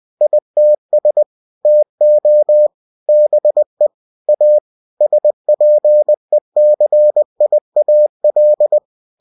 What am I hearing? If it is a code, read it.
Morse code, heard as a keyed tone, about 20 wpm, 600 Hz. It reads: ITS TO BE A SPECIAL
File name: cw-books / winnie-the-pooh / winnie-20wpm-page-0102